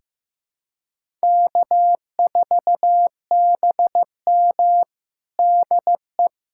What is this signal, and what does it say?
Morse code, heard as a keyed tone, about 15 wpm, 700 Hz. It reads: K4BM DE
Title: USA Callsign DE Callsign - 4 Characters - 15wpm